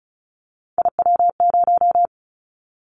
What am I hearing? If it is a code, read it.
Morse code, heard as a keyed tone, about 36 wpm, 700 Hz. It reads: IW0
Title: Callsign Prefixes for Top 25 DXCC - 35wpm